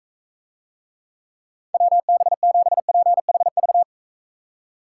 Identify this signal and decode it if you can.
Morse code, heard as a keyed tone, about 43 wpm, 700 Hz. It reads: WB7PHV